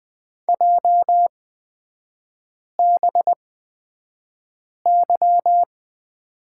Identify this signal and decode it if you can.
Morse code, heard as a keyed tone, about 20 wpm, spaced as for 5 wpm, 700 Hz. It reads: JBY
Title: 224 Lesson 36e - BT - Three Characters 20wpm